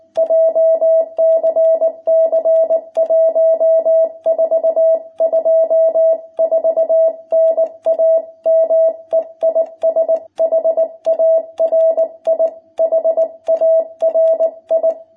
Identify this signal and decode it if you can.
Morse code, heard as a keyed tone, about 19 wpm, 660 Hz. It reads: JCC 1424 NAME IS HARI HARI
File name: jh8cbh / 2021cw